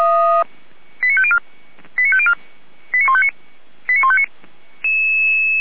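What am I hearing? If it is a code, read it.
Five-tone selective call (ZVEI), heard as quick runs of five tones, tones of about 70 ms. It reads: 89504 89504 89159 89159
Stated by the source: ZVEI five-tone selective call received on an SDR by fire brigade Feuerwehr Orlamünde.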